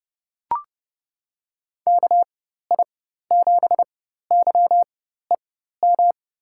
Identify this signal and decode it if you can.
Morse code, heard as a keyed tone, about 30 wpm, spaced as for 14 wpm, 700 Hz. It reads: KI7YEM
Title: USA Call Signs - 6 Characters - 4x Character Spacing - 30wpm